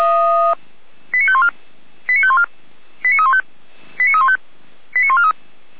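Five-tone selective call (ZVEI), heard as quick runs of five tones, tones of about 70 ms. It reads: 89514 89514 89315 89315 89143
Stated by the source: Five-tone ZVEI selective call 89265 received on an SDR.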